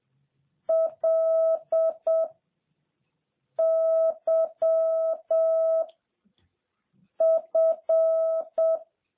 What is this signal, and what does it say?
Morse code, heard as a keyed tone, about 7 wpm, 650 Hz. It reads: L Y F
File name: jh8cbh / 2021cw